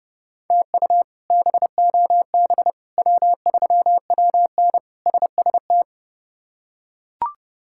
Morse code, heard as a keyed tone, about 30 wpm, 700 Hz. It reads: TU BOB W3WN SST
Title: K1USN SST QSO Practice - 30wpm